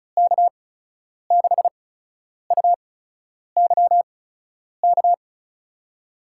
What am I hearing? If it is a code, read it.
Morse code, heard as a keyed tone, about 36 wpm, spaced as for 10 wpm, 700 Hz. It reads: K6UYK
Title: USA Call Signs - 6 Characters - 8x Character Spacing - 35wpm